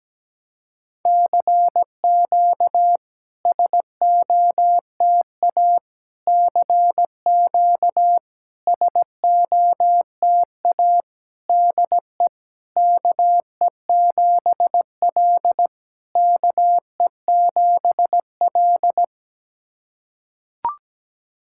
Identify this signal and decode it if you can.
Morse code, heard as a keyed tone, about 17 wpm, 700 Hz. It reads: CQ SOTA CQ SOTA DE KE7L KE7L
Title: SOTA QSO Practice 17wpm